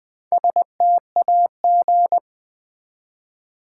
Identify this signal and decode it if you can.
Morse code, heard as a keyed tone, about 20 wpm, 700 Hz. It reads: STAG
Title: Build A Word - 1 Letter At A Time - 5-Letter Words - 20wpm